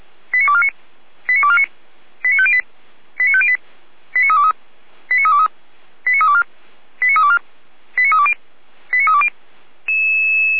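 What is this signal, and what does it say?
Five-tone selective call (ZVEI), heard as quick runs of five tones, tones of about 70 ms. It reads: 89249 89249 89508 89508 89323 89323 89324 89324 89320 89320
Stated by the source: Five-tone ZVEI selective call 89250 received by SDR, from the feuerwehr-orlamuende fire brigade server.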